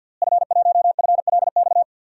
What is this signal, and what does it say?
Morse code, heard as a keyed tone, about 46 wpm, 700 Hz. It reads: F1FLX
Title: DXCC-100 Call Signs - 5 Characters - Repeated 3x - 50wpm